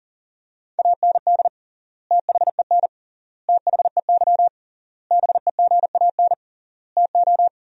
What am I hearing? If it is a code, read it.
Morse code, heard as a keyed tone, about 40 wpm, 700 Hz. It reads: AND THEN THEY BEGAN TO
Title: 3x Word-Spacing - Sentences from Top 300 Words - 40wpm